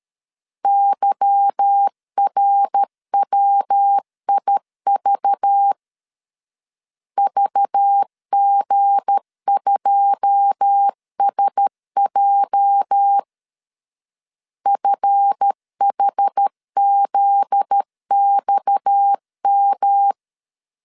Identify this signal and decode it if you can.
Morse code, heard as a keyed tone, about 13 wpm, 800 Hz. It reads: YRWIV VG2SJ FHZXM
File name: CW / DET / 1 9 1500C S C 12wpm 3811